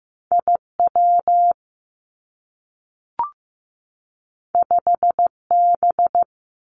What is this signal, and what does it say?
Morse code, heard as a keyed tone, about 15 wpm, 700 Hz. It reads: I W 5 B
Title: Two Challenging Letter-Number Combinations 15wpm